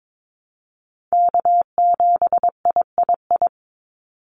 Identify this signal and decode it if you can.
Morse code, heard as a keyed tone, about 22 wpm, 700 Hz. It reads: K7III
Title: NA SOTA Chaser Callsigns 22wpm